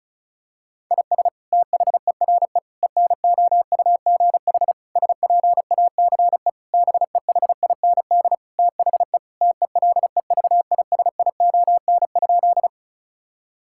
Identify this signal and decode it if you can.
Morse code, heard as a keyed tone, about 36 wpm, 700 Hz. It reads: IS THERE ENOUGH SPACE BEHIND THE TELEVISION?
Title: Sentences from Top 1000 Words - Review Part 5 - 35wpm